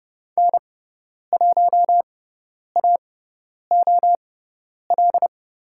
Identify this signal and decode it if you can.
Morse code, heard as a keyed tone, about 30 wpm, spaced as for 10 wpm, 700 Hz. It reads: N1AOL
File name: Lesson 18g - Letter m - Call Signs 30wpm